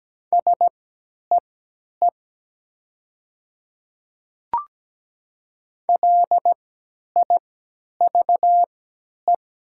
Morse code, heard as a keyed tone, about 17 wpm, spaced as for 10 wpm, 700 Hz. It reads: SEE LIVE
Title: 3x Character Spacing - Top 500 Words - 17wpm